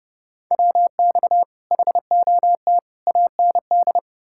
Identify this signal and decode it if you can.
Morse code, heard as a keyed tone, about 40 wpm, 700 Hz. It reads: WX HOT AND